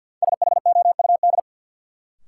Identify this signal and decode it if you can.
Morse code, heard as a keyed tone, about 51 wpm, 700 Hz. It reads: SHOUD
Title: Sentences from Top 1000 Words - Review Part 4 - New Word Drill - 50wpm